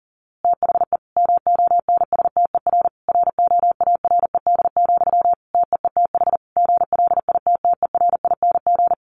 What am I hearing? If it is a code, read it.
Morse code, heard as a keyed tone, about 40 wpm, 700 Hz. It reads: THE MONSTER ROARED, TEETH GLITTERING
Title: A Sound Of Thunder 40wpm